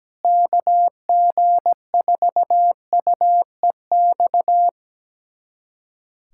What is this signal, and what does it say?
Morse code, heard as a keyed tone, about 17 wpm, 700 Hz. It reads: KG4UEX
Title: USA Callsign DE Callsign - 6 Characters - 17wpm